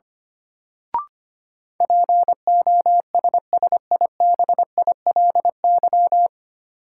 Morse code, heard as a keyed tone, about 25 wpm, 700 Hz. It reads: POSSIBILY
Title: Sentences from Top 1300 Words - New Word Drill - 25wpm